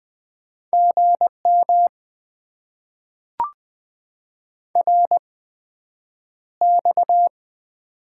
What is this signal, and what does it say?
Morse code, heard as a keyed tone, about 20 wpm, spaced as for 9 wpm, 700 Hz. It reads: GM RX
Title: Top 90 QSO Elements - 8x Character Spacing - 20wpm